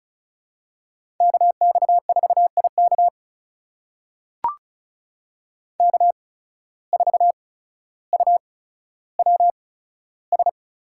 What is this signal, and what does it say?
Morse code, heard as a keyed tone, about 36 wpm, spaced as for 15 wpm, 700 Hz. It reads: KX4IK K4UWS
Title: USA Call Signs - 5 Characters - 8x Character Spacing - 35wpm